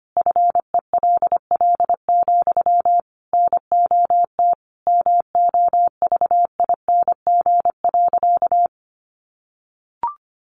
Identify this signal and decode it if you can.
Morse code, heard as a keyed tone, about 25 wpm, 700 Hz. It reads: FELL, NOT MOVING.
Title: A Sound Of Thunder 25wpm